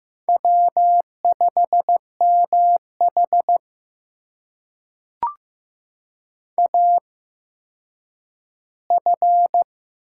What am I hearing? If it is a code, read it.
Morse code, heard as a keyed tone, about 15 wpm, spaced as for 9 wpm, 700 Hz. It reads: W5MH AF
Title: USA Call Signs - 4 Characters - 8x Character Spacing - 15wpm